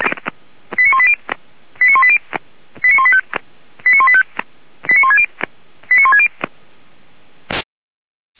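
Five-tone selective call (ZVEI), heard as quick runs of five tones, tones of about 80 ms. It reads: 89180 89180 89185 89185 89160 89160
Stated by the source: ZVEI five-tone selective call received by SDR.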